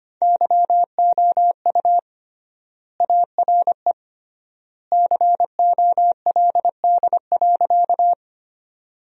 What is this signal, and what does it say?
Morse code, heard as a keyed tone, about 25 wpm, 700 Hz. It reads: YOU ARE COLD.
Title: 3x Word-Spacing - Sentences from Top 400 Words - 25wpm